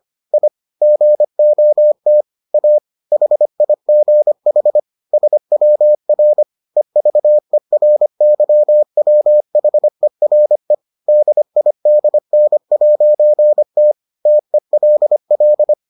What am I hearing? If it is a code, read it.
Morse code, heard as a keyed tone, about 25 wpm, 600 Hz. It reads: I GOT A HIGH SWR EVERYWHERE DIDN'T TELL